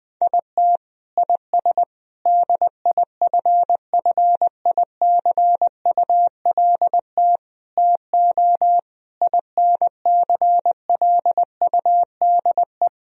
Morse code, heard as a keyed tone, about 20 wpm, 700 Hz. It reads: IT IS DIFFICULT TO INCLUDE
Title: Sentences from Top 1000 Words - Review Part 3 - 20wpm